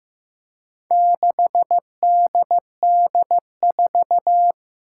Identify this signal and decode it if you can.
Morse code, heard as a keyed tone, about 15 wpm, 700 Hz. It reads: 6DD4
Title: Four Challenging Letter-Number Combinations 15wpm